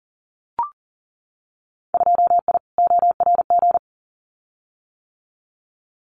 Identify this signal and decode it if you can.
Morse code, heard as a keyed tone, about 38 wpm, 700 Hz. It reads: 2I ORG
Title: Winter Field Day Exchanges Encore 40wpm